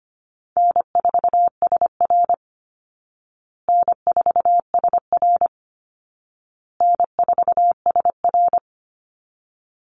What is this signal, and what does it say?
Morse code, heard as a keyed tone, about 25 wpm, 700 Hz. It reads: N4SR N4SR N4SR
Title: Easy Call Signs Repeated 3x - 25wpm